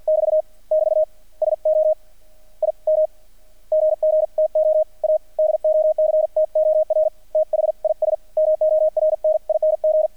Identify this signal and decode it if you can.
Morse code, heard as a keyed tone, about 48 wpm, 620 Hz, with radio noise. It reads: <BT> <BT> SO IM GG TO A DOCTOR THIS MORNING
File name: archived / CRSnet